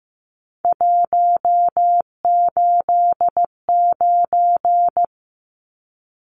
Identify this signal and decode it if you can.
Morse code, heard as a keyed tone, about 15 wpm, 700 Hz. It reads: 189
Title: Numbers - 3-Digit 15wpm